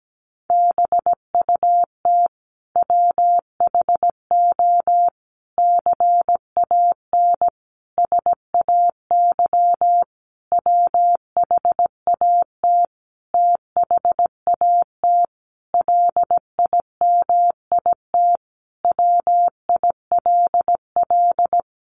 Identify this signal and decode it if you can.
Morse code, heard as a keyed tone, about 17 wpm, 700 Hz. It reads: BUT WHO CAN SAY WHAT THAT LIMIT WILL
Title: Sentences from Top 1500 Words 17wpm